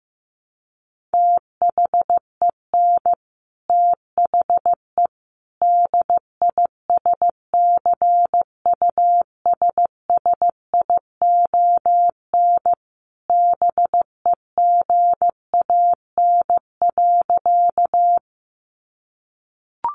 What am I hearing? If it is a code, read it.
Morse code, heard as a keyed tone, about 15 wpm, 700 Hz. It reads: THEN THE DISCUSSION BEGAN.
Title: Sentences from Top 1100 Words 15wpm